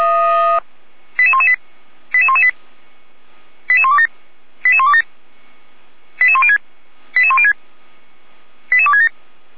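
Five-tone selective call (ZVEI), heard as quick runs of five tones, tones of about 70 ms. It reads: 88118 88118 88127 88127 88106 88106 88267
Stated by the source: ZVEI five-tone selective call received on SDR by feuerwehr-orlamuende.org (fire brigade Orlamünde).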